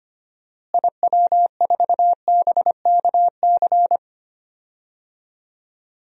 Morse code, heard as a keyed tone, about 25 wpm, 700 Hz. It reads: IW4BKC
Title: DXCC-100 Call Signs - 6 Characters - Repeated 2x - 25wpm